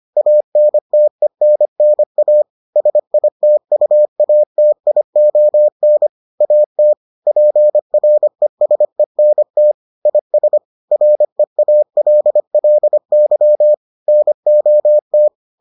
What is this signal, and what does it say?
Morse code, heard as a keyed tone, about 25 wpm, 600 Hz. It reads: ANTENNA SITUATION AT PRESENT IS REALLY NOT